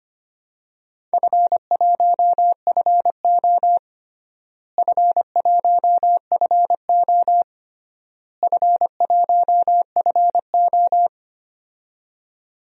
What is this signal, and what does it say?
Morse code, heard as a keyed tone, about 25 wpm, 700 Hz. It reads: F1FO F1FO F1FO
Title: DXCC-100 Call Signs - 4 Characters - Repeated 3x - 25wpm